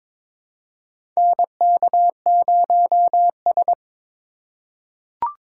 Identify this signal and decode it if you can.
Morse code, heard as a keyed tone, about 22 wpm, 700 Hz. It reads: NK0S
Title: USA Call Signs - 4 Characters - 6x Character Spacing - 22wpm